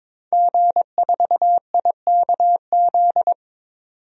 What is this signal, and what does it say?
Morse code, heard as a keyed tone, about 21 wpm, 700 Hz. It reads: G4IKZ
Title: DXCC-100 Call Signs - 5 Characters - 22wpm